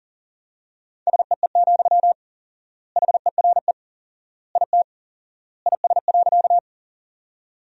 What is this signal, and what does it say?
Morse code, heard as a keyed tone, about 40 wpm, 700 Hz. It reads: SEE, HERE IT IS.
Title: 4x Word-Spacing - Sentences from Top 200 Words - 40wpm